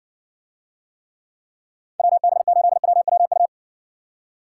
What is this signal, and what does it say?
Morse code, heard as a keyed tone, about 60 wpm, 700 Hz. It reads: WB8WWU